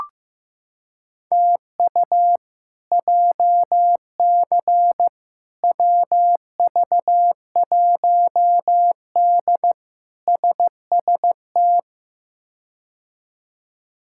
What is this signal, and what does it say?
Morse code, heard as a keyed tone, about 15 wpm, 700 Hz. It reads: TU JC WV1D SST
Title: K1USN SST QSO Practice - 15wpm